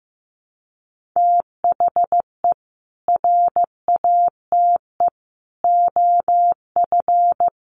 Morse code, heard as a keyed tone, about 15 wpm, 700 Hz. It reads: THE RATE OF
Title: Sentences from Top 1100 Words 15wpm